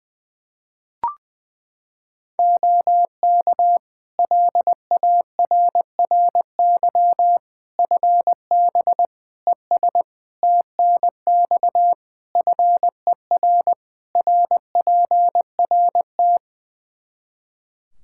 Morse code, heard as a keyed tone, about 20 wpm, 700 Hz. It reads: OK LARRY FB ES TNX FER RPRT
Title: Practice QSO - Exchange 2 - Info - 20wpm